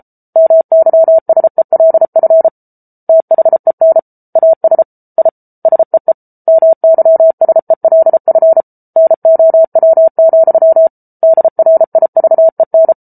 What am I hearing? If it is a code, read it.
Morse code, heard as a keyed tone, about 34 wpm, 650 Hz. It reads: MYSELF THEN AS I SEE MYSELF NOW, DRIVEN